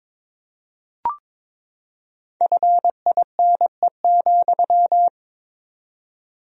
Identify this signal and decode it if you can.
Morse code, heard as a keyed tone, about 22 wpm, 700 Hz. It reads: FINE,
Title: 4x Word-Spacing - Sentences from Top 500 Words - 22wpm